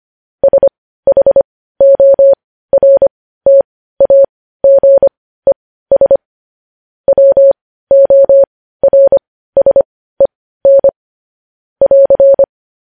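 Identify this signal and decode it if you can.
Morse code, heard as a keyed tone, about 25 wpm, spaced as for 15 wpm, 550 Hz. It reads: SHORTAGES WORSEN <AR>
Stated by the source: Morse news headlines practice at 15 wpm.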